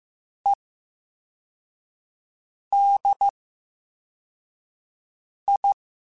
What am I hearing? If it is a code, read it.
Morse code, heard as a keyed tone, about 15 wpm, spaced as for 4 wpm, 800 Hz. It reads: EDI
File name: Special INGs / Special INGs-mp3s / Slow ing @13